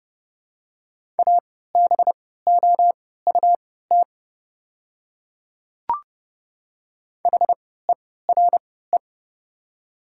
Morse code, heard as a keyed tone, about 30 wpm, spaced as for 17 wpm, 700 Hz. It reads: ABOUT HERE
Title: 3x Character Spacing - Top 300 Words - 30wpm